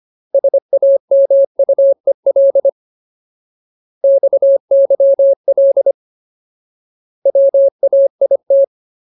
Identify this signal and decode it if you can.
Morse code, heard as a keyed tone, about 25 wpm, 550 Hz. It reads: SAMUEL XYL WAIT
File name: Ultimate Word List / 25WPM Ultimate